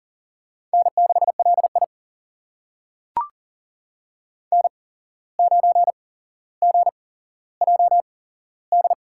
Morse code, heard as a keyed tone, about 40 wpm, spaced as for 16 wpm, 700 Hz. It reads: N6LI N9GJD
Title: USA Call Signs - 8x Character Spacing - 40wpm